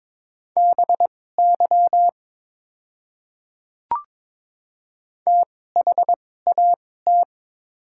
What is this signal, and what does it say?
Morse code, heard as a keyed tone, about 22 wpm, spaced as for 16 wpm, 700 Hz. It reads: BY THAT
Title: 2x Character Spacing - Top 100 Words - 22wpm